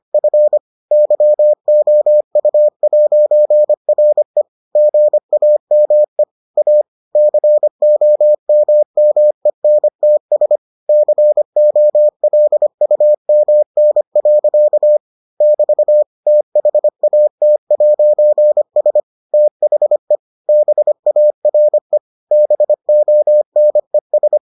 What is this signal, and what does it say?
Morse code, heard as a keyed tone, about 25 wpm, 600 Hz. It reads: F YOU'RE GAME A COMMENTS COLUMN. <BT> THAT'S THE BARE BONES